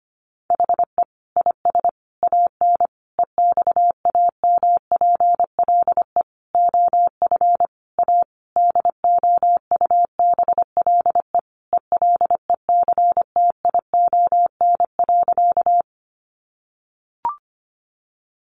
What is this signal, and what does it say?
Morse code, heard as a keyed tone, about 25 wpm, 700 Hz. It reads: HE IS AN EXAMPLE OF A DOUBLE ELECTION.
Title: Sentences from Top 1500 Words 25wpm